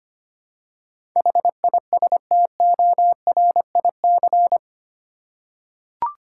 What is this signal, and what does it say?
Morse code, heard as a keyed tone, about 25 wpm, 700 Hz. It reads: HISTORIC
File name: 8 Letter Words - First 3 Letters Given - 25wpm